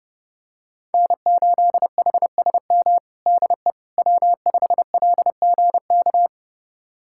Morse code, heard as a keyed tone, about 29 wpm, 700 Hz. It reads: N8HSM DE W5LGK
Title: USA Callsign DE Callsign - 5 Characters - 30wpm